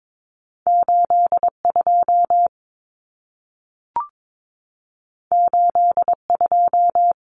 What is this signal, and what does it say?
Morse code, heard as a keyed tone, about 22 wpm, 700 Hz. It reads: 82 82
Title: Common Cut Numbers - 2-Digit 22wpm